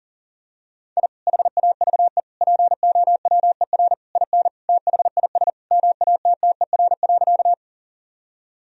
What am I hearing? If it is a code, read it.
Morse code, heard as a keyed tone, about 40 wpm, 700 Hz. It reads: I HAVE POWER IN THIS MATTER.
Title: Sentences from Top 1000 Words - Review Part 2 - 40wpm